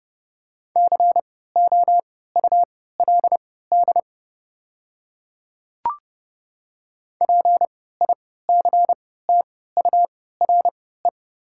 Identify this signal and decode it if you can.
Morse code, heard as a keyed tone, about 30 wpm, spaced as for 17 wpm, 700 Hz. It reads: COULD PICTURE
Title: 3x Character Spacing - Top 200 Words - 30wpm